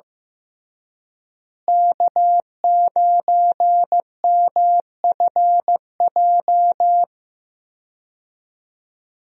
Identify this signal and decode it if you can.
Morse code, heard as a keyed tone, about 15 wpm, 700 Hz. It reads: K9MFJ
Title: USA Call Signs - 5 Characters - Repeated 3x - 15wpm